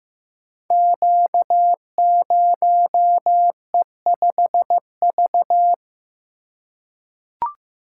Morse code, heard as a keyed tone, about 15 wpm, 700 Hz. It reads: Q0E5V
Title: Code Groups - 5 Letter:Numbers - 15wpm